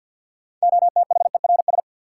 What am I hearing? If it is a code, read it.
Morse code, heard as a keyed tone, about 53 wpm, 700 Hz. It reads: OTHERS
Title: Sentences from Top 1500 Words - New Word Drill Alt - 50wpm